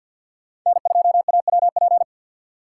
Morse code, heard as a keyed tone, about 53 wpm, 700 Hz. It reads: N2AWP
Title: USA Call Signs - 5 Characters - 8x Character Spacing - 50wpm